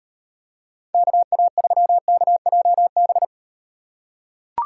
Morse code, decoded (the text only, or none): KA3KJB